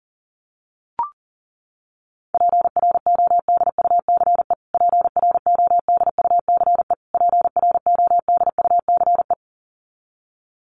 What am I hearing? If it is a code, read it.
Morse code, heard as a keyed tone, about 40 wpm, 700 Hz. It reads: PRODUCE PRODUCE PRODUCE
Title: Top 500 Words Repeated 40wpm